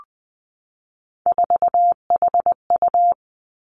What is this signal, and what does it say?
Morse code, heard as a keyed tone, about 20 wpm, 700 Hz. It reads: 4HU
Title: Three Challenging Letter-Number Combinations 20wpm